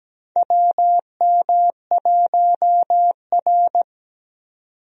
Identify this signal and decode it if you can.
Morse code, heard as a keyed tone, about 17 wpm, 700 Hz. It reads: WM1R
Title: USA Call Signs - 4 Characters - 17wpm